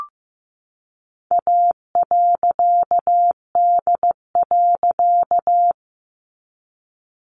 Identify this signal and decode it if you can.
Morse code, heard as a keyed tone, about 15 wpm, 700 Hz. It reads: A.D.
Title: A Sound Of Thunder 15wpm